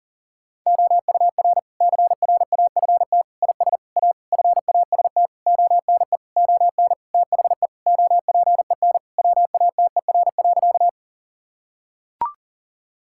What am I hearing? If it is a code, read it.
Morse code, heard as a keyed tone, about 40 wpm, 700 Hz. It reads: OUR CRAFT IS A FAST ONE ON THE OPEN WATER.
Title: Sentences from Top 1000 Words - Review Part 6 - 40wpm